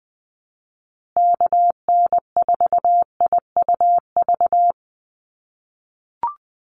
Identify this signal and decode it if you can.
Morse code, heard as a keyed tone, about 20 wpm, 700 Hz. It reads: KN4IUV
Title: USA Call Signs Repeated 3x 20wpm